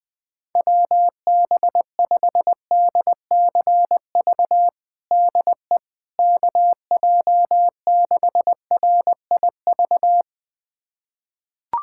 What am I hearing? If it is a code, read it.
Morse code, heard as a keyed tone, about 20 wpm, 700 Hz. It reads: WB5DCV DE KJ6RIV
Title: USA Callsign DE Callsign - 6 Characters - 20wpm